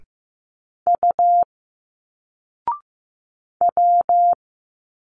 Morse code, heard as a keyed tone, about 15 wpm, 700 Hz. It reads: U W
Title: Single Letters 15wpm